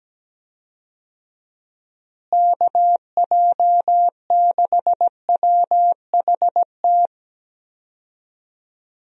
Morse code, heard as a keyed tone, about 17 wpm, 700 Hz. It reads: KJ6WHT
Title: USA Call Signs - 6 Characters - Repeated 3x - 17wpm